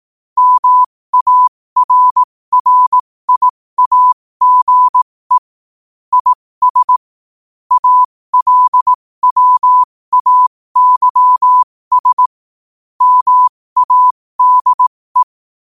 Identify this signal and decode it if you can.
Morse code, heard as a keyed tone, about 18 wpm, spaced as for 15 wpm, 1000 Hz. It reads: MARRIAGE IS ALWAYS MADE
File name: qotdfiles / 15wpm